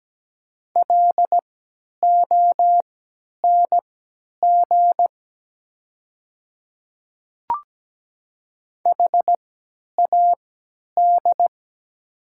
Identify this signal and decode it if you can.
Morse code, heard as a keyed tone, about 17 wpm, spaced as for 10 wpm, 700 Hz. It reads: LONG HAD
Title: 3x Character Spacing - Top 100 Words - 17wpm